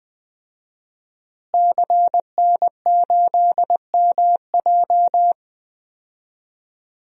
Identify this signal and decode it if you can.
Morse code, heard as a keyed tone, about 21 wpm, 700 Hz. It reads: CN8MJ
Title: DXCC-100 Call Signs - 5 Characters - Repeated 2x - 20wpm